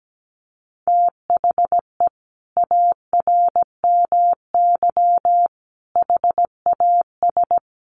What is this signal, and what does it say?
Morse code, heard as a keyed tone, about 17 wpm, 700 Hz. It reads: THE ARMY HAS
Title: Sentences from Top 800 Words 17wpm